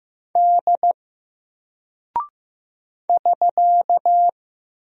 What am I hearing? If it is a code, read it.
Morse code, heard as a keyed tone, about 15 wpm, 700 Hz. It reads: D <SK>
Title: Single Character (Letter:Number:Punctuation:Prosign) Alt 15wpm